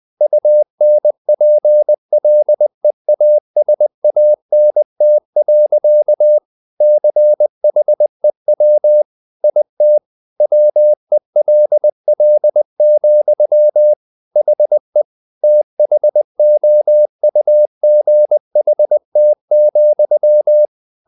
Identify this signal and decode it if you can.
Morse code, heard as a keyed tone, about 20 wpm, 600 Hz. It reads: UNPLEASANT. CHEW IT WELL, HE THOUGHT,
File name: the old man and the sea / the old man and the sea part 1 20wpm